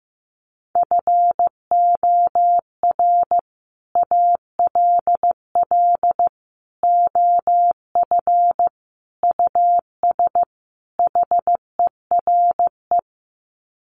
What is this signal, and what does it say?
Morse code, heard as a keyed tone, about 15 wpm, 700 Hz. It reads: FOR ALL OF US HERE
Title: Sets of 5 Words 15wpm